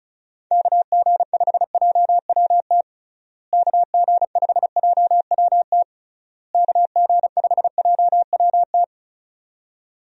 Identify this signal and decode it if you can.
Morse code, heard as a keyed tone, about 36 wpm, 700 Hz. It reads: KG5JWT KG5JWT KG5JWT
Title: USA Call Signs - 6 Characters - Repeated 3x - 35wpm